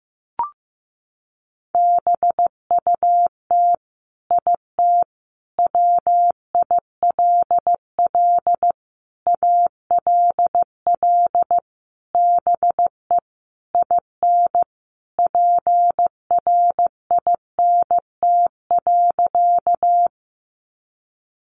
Morse code, heard as a keyed tone, about 15 wpm, 700 Hz. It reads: BUT IT WILL ALL BE IN PRINT.
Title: Sentences from Top 1400 Words 15wpm